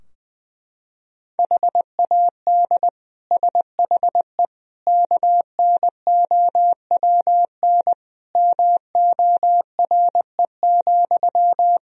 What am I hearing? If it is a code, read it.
Morse code, heard as a keyed tone, about 20 wpm, 700 Hz. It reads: HAD SHE KNOWN MORE,
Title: Sentences from Top 1000 Words - Review Part 3 - 20wpm